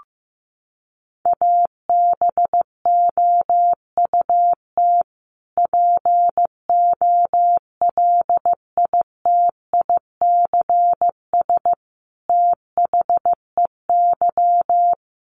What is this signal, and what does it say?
Morse code, heard as a keyed tone, about 14 wpm, 700 Hz. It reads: ABOUT POLITICS THEY
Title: Sentences from Top 1000 Words 15wpm